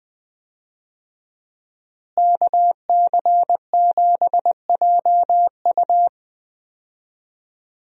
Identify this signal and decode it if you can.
Morse code, heard as a keyed tone, about 20 wpm, 700 Hz. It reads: KC7JU
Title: USA Call Signs - 5 Characters - Repeated 2x - 20wpm